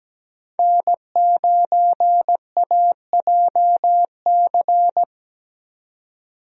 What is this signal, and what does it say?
Morse code, heard as a keyed tone, about 17 wpm, 700 Hz. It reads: N9AJC
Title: USA Call Signs - 5 Characters - 17wpm